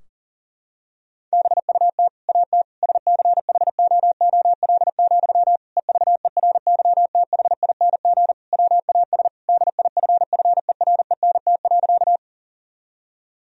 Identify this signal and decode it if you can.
Morse code, heard as a keyed tone, about 40 wpm, 700 Hz. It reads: BUT AT SCHOOL, EVERYTHING WAS DIFFERENT.
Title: Sentences from Top 1000 Words - Review Part 3 - 40wpm